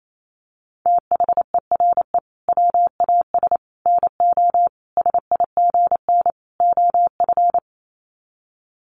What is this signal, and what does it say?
Morse code, heard as a keyed tone, about 28 wpm, 700 Hz. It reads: THERE WAS NO SIGN OF
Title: Sets of 5 Words Encore2 28wpm